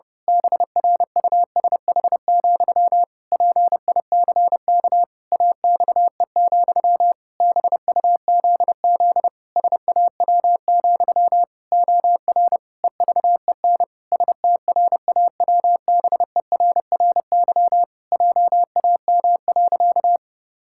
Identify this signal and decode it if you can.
Morse code, heard as a keyed tone, about 30 wpm, 700 Hz. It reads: BRUSH, PICK AXE, BUZZ SAW, OR EVEN STRAWBERRY JAM.
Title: Pangram Challenge v2 - 30wpm